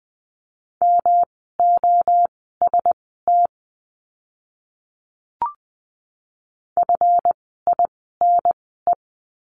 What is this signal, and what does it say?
Morse code, heard as a keyed tone, about 20 wpm, spaced as for 14 wpm, 700 Hz. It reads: MOST FINE
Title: 139 Lesson 23f - Letter f - Words 20wpm